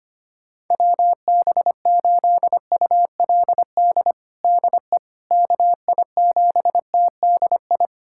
Morse code, heard as a keyed tone, about 25 wpm, 700 Hz. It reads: WB8ULD DE KI7TDI
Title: USA Callsign DE Callsign - 6 Characters - 25wpm